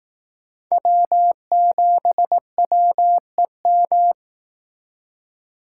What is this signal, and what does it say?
Morse code, heard as a keyed tone, about 18 wpm, 700 Hz. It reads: W7WEM
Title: Speed Racing - USA 1x3 Call Signs 15wpm